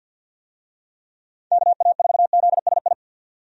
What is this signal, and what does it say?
Morse code, heard as a keyed tone, about 52 wpm, 700 Hz. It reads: KA4ZSI
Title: USA Call Signs - 6 Characters - 50wpm